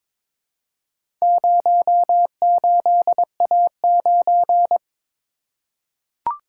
Morse code, heard as a keyed tone, about 22 wpm, 700 Hz. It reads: 08A9
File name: practice / 22wpm / Four Letter-Number Combinations - 22wpm